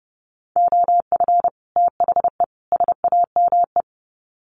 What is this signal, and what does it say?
Morse code, heard as a keyed tone, about 30 wpm, 700 Hz. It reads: OF THE SAME
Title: Sets of 3 Words 30wpm